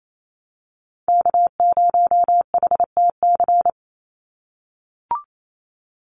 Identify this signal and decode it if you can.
Morse code, heard as a keyed tone, about 28 wpm, 700 Hz. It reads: K0HTC
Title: USA Call Signs 28wpm